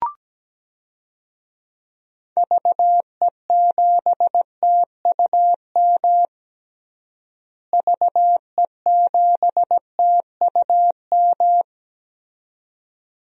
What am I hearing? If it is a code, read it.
Morse code, heard as a keyed tone, about 17 wpm, 700 Hz. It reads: VE7TUM VE7TUM
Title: DXCC-100 Call Signs - 6 Characters - Repeated 2x - 17wpm